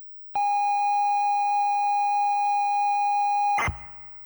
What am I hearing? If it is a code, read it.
ZX Spectrum tape loading.